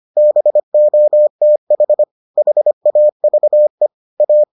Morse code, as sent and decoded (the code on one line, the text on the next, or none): -... --- - .... / .... .- ...- . / .-
BOTH HAVE A